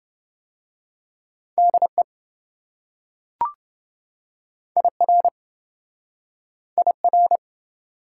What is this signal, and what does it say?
Morse code, heard as a keyed tone, about 27 wpm, 700 Hz. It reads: DE IR IR